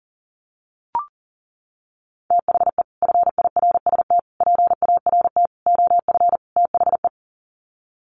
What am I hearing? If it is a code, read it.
Morse code, heard as a keyed tone, about 37 wpm, 700 Hz. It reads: THE FIRST PART OF THE
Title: Sets of 5 Words Encore 40wpm